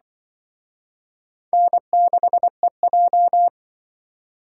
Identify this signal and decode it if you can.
Morse code, heard as a keyed tone, about 24 wpm, 700 Hz. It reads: N6EJ
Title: Speed Racing - USA Call Signs 20wpm